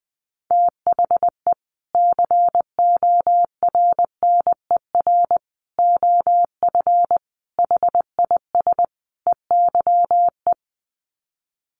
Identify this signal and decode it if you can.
Morse code, heard as a keyed tone, about 20 wpm, 700 Hz. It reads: THE CORNER OF HIS EYE
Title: Sets of 5 Words Encore2 20wpm